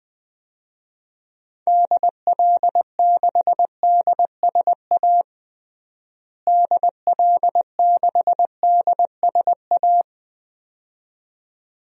Morse code, heard as a keyed tone, about 20 wpm, 700 Hz. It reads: DL6DSA DL6DSA
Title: DXCC-100 Call Signs - 6 Characters - Repeated 2x - 20wpm